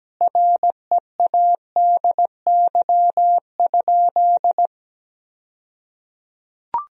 Morse code, heard as a keyed tone, about 17 wpm, 700 Hz. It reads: READY?
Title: 2x Word-Spacing - Sentences from Top 400 Words - 17wpm